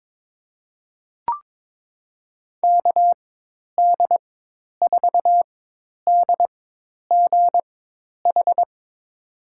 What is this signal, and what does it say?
Morse code, heard as a keyed tone, about 22 wpm, spaced as for 10 wpm, 700 Hz. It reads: KD4DGH
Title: USA Call Signs - 6 Characters - 4x Character Spacing - 22wpm